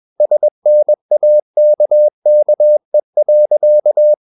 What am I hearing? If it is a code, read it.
Morse code, heard as a keyed tone, about 21 wpm, 600 Hz. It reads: SNAKKE.
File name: Ferie QRS 21wpm